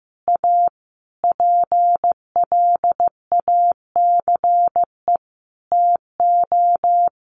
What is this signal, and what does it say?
Morse code, heard as a keyed tone, about 15 wpm, 700 Hz. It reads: A PLACE TO
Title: Sentences from Top 600 Words 15wpm